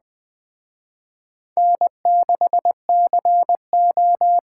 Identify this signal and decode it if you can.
Morse code, heard as a keyed tone, about 21 wpm, 700 Hz. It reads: N6CO